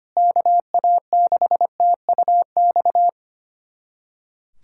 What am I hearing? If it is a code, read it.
Morse code, heard as a keyed tone, about 25 wpm, 700 Hz. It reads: KA6TUX
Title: USA Callsign DE Callsign 25wpm